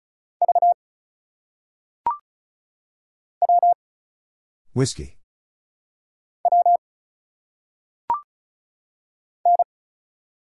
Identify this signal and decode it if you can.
Morse code, heard as a keyed tone, about 34 wpm, 700 Hz. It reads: V W W N
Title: Single Letters - Phonetic Pronunciation 35wpm